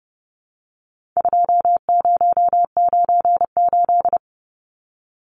Morse code, heard as a keyed tone, about 30 wpm, 700 Hz. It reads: 2098
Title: Numbers - 4-Digit 30wpm